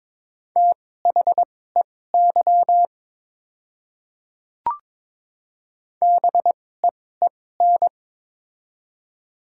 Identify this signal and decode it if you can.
Morse code, heard as a keyed tone, about 22 wpm, spaced as for 16 wpm, 700 Hz. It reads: THEY BEEN
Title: 2x Character Spacing - Top 100 Words - 22wpm